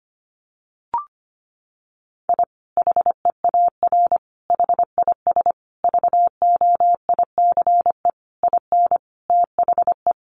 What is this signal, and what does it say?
Morse code, heard as a keyed tone, about 25 wpm, 700 Hz. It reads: I HEAR HIS VOICE IN THE
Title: Sentences from Top 500 Words 25wpm